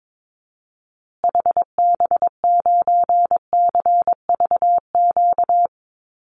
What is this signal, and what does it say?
Morse code, heard as a keyed tone, about 22 wpm, 700 Hz. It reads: HB9CVQ
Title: Call Signs Repeated 2x - 22wpm